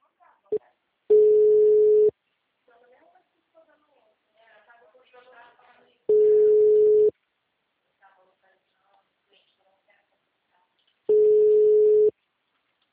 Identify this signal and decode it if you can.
Ringback tone (Europe, 425 Hz), with speech.